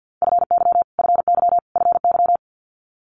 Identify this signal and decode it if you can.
Morse code, heard as a keyed tone, about 50 wpm, 700 Hz. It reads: FY FY FY
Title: Multi-letter Phonemes Repeated 3x 50wpm